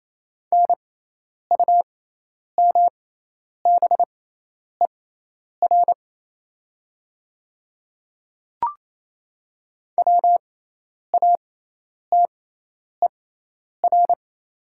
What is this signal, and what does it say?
Morse code, heard as a keyed tone, about 28 wpm, spaced as for 10 wpm, 700 Hz. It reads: NUMBER WATER
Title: 6x Character Spacing - Top 100 Words - 28wpm